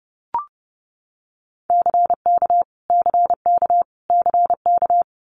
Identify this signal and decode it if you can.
Morse code, heard as a keyed tone, about 30 wpm, 700 Hz. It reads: CK CK CK
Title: Multi-letter Phonemes Repeated 3x 30wpm